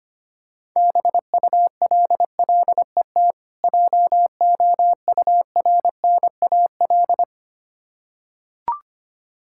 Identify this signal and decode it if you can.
Morse code, heard as a keyed tone, about 25 wpm, 700 Hz. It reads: BULLET JOURNAL
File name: practice / 25wpm / Difficult Words with Hints - 25wpm